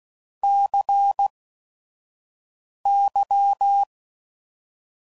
Morse code, heard as a keyed tone, about 16 wpm, spaced as for 5 wpm, 800 Hz. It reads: CY
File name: CW Code Groups @5WPM / Technical Subjects / XC